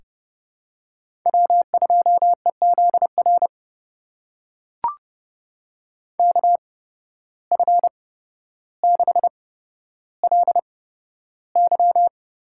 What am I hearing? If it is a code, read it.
Morse code, heard as a keyed tone, about 32 wpm, spaced as for 13 wpm, 700 Hz. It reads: W2EZR KF6LY
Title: USA Call Signs - 5 Characters - 8x Character Spacing - 30wpm